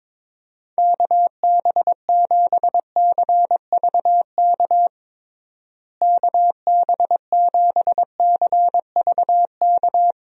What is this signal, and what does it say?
Morse code, heard as a keyed tone, about 22 wpm, 700 Hz. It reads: KB7CVK KB7CVK
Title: USA Call Signs - 6 Characters - Repeated 2x - 22wpm